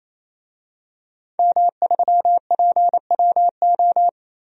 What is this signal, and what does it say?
Morse code, heard as a keyed tone, about 28 wpm, 700 Hz. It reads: M3PWO
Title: DXCC-100 Call Signs - 5 Characters - 28wpm